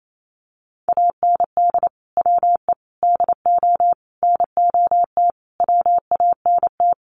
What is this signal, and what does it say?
Morse code, heard as a keyed tone, about 28 wpm, 700 Hz. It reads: AND WE DO NOT WANT
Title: Sets of 5 Words Encore2 28wpm